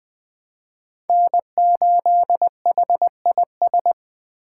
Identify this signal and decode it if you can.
Morse code, heard as a keyed tone, about 21 wpm, 700 Hz. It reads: N8HIS